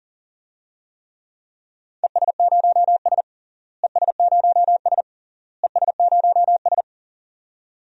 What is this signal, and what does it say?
Morse code, heard as a keyed tone, about 40 wpm, 700 Hz. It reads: ES0S ES0S ES0S
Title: DXCC-100 Call Signs - 4 Characters - Repeated 3x - 40wpm